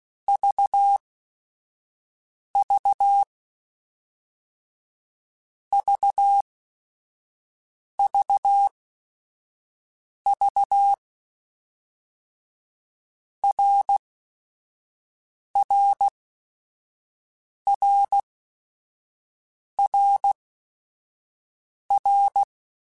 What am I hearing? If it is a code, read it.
Morse code, heard as a keyed tone, about 16 wpm, 800 Hz. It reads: V V V V V R R R R R
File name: Basic Sets / Basic Letter Groups Collective7 / RKPX@5